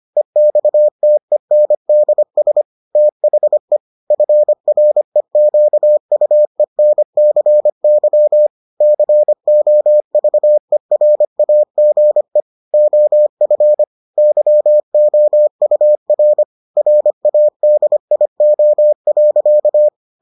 Morse code, heard as a keyed tone, about 25 wpm, 600 Hz. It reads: EXTENDS THE FREQUENCY COVERAGE OF YOUR RADIO.